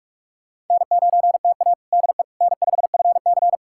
Morse code, heard as a keyed tone, about 46 wpm, 700 Hz. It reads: N9TA DE N5FC
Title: USA Callsign DE Callsign - 4 Characters - 45wpm